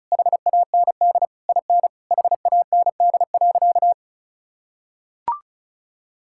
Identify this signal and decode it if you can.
Morse code, heard as a keyed tone, about 36 wpm, 700 Hz. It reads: HAND IN HAND.
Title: Sentences from Top 1000 Words - Review Part 5 - 35wpm